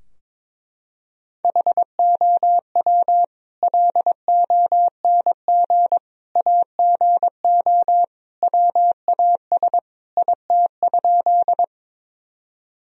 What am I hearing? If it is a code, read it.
Morse code, heard as a keyed tone, about 21 wpm, 700 Hz. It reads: HOW LONG AGO WAS IT?
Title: Sentences from Top 1000 Words - Review Part 3 - 22wpm